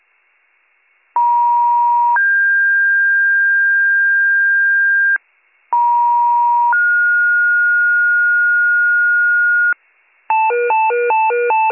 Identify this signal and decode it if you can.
Two-tone page, heard as a one-second tone, then a three-second tone, about 950 then 1600 Hz, about 950 then 1430 Hz, with noise. It is heard from 1.2 to 9.7 s.